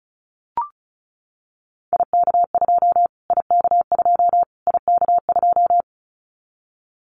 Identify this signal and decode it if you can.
Morse code, heard as a keyed tone, about 34 wpm, 700 Hz. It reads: IK2 IK2 IK2
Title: Callsign Prefixes for Top 25 DXCC - Repeated 3x - 35wpm